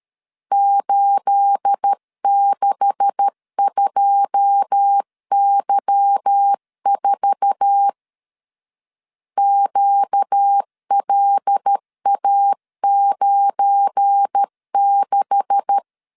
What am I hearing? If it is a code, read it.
Morse code, heard as a keyed tone, about 13 wpm, 800 Hz. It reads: 862Y4 QLA96
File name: LEZIONI / 9 286 12wpm 2305